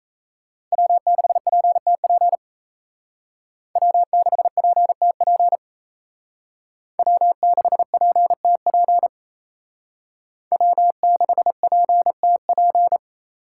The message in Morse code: W6PTP W6PTP W6PTP W6PTP

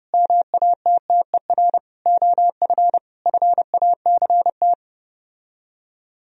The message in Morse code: MATTER OF FACT

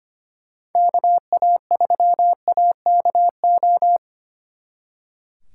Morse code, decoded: KA3AKO

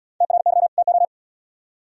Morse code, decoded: TRY TO

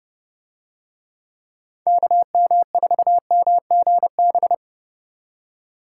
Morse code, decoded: KM4MGB

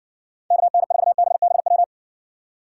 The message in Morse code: KN4BBW

